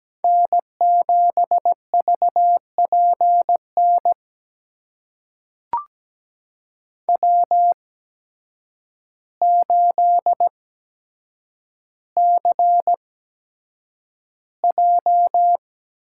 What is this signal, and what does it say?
Morse code, heard as a keyed tone, about 17 wpm, spaced as for 8 wpm, 700 Hz. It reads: N7VPN W8CJ
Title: USA Call Signs - 5 Characters - 8x Character Spacing - 17wpm